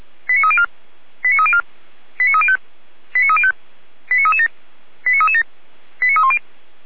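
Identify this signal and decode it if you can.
Five-tone selective call (ZVEI), heard as quick runs of five tones, tones of about 70 ms. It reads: 89304 89304 89305 89305 89337 89337 89310
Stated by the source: ZVEI five-tone call 89262 received by SDR, archived by Feuerwehr Orlamünde fire brigade.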